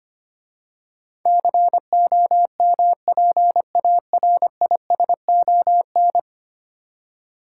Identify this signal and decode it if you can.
Morse code, heard as a keyed tone, about 25 wpm, 700 Hz. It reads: COMPARISON
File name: 10 Letter Words - First 3 Letters Given - 25wpm